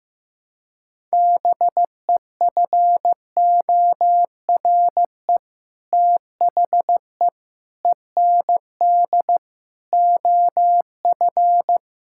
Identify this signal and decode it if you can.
Morse code, heard as a keyed tone, about 16 wpm, 700 Hz. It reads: BEFORE THE END OF
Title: Sentences from Top 1000 Words - Review Part 5 - 15wpm